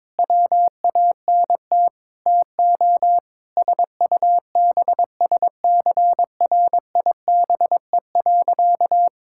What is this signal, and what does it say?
Morse code, heard as a keyed tone, about 22 wpm, 700 Hz. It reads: WANT TO SUBSCRIBE.
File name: practice / 22wpm / Sentences with 8-Word Phrases - 22wpm